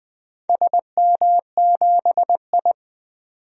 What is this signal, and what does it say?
Morse code, heard as a keyed tone, about 20 wpm, 700 Hz. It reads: SM7I
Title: DXCC-100 Call Signs - 4 Characters - 20wpm